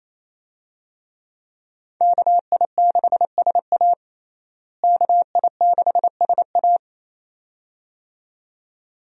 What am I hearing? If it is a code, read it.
Morse code, heard as a keyed tone, about 28 wpm, 700 Hz. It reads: KI6SA KI6SA
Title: USA Call Signs - 5 Characters - Repeated 2x - 28wpm